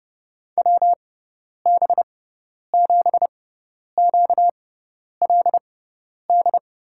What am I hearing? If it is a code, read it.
Morse code, heard as a keyed tone, about 30 wpm, spaced as for 10 wpm, 700 Hz. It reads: WB7QLD
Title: USA Call Signs - 6 Characters - 6x Character Spacing - 30wpm